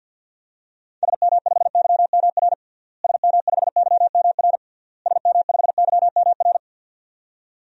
Morse code, heard as a keyed tone, about 50 wpm, 700 Hz. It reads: SM5YMR SM5YMR SM5YMR
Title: DXCC-100 Call Signs - 6 Characters - Repeated 3x - 50wpm